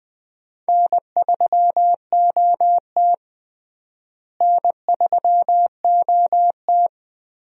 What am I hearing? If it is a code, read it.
Morse code, heard as a keyed tone, about 20 wpm, 700 Hz. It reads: N3OT N3OT
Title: USA Call Signs - 4 Characters - Repeated 3x - 20wpm